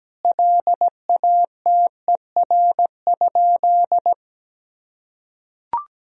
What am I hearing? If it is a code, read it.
Morse code, heard as a keyed tone, about 17 wpm, 700 Hz. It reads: LATER?
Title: Sentences from Top 1000 Words - Review Part 4 - 17wpm